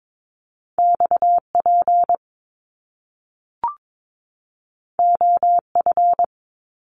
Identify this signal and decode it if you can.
Morse code, heard as a keyed tone, about 22 wpm, 700 Hz. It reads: XP OF
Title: Top Two Letter Combinations in Words 22wpm